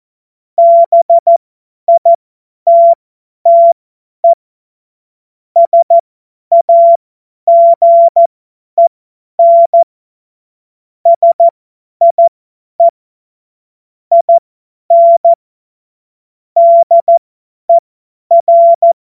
Morse code, heard as a keyed tone, about 14 wpm, spaced as for 10 wpm, 680 Hz. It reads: BITTE SAGEN SIE IN DER
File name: Morsekurs / Witze / Witz 20. Dezember 2022 50 F70